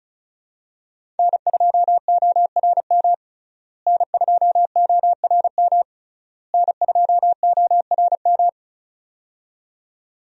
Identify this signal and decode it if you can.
Morse code, heard as a keyed tone, about 36 wpm, 700 Hz. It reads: N2ORM N2ORM N2ORM
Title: USA Call Signs - 5 Characters - Repeated 3x - 35wpm